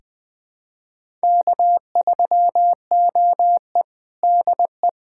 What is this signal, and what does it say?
Morse code, heard as a keyed tone, about 21 wpm, 700 Hz. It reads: K3OE DE
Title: USA Callsign DE Callsign - 4 Characters - 20wpm